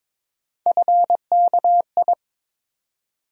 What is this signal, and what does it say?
Morse code, heard as a keyed tone, about 22 wpm, 700 Hz. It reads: FKI
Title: USA Call Sign Postfixes - 3 Letters 22wpm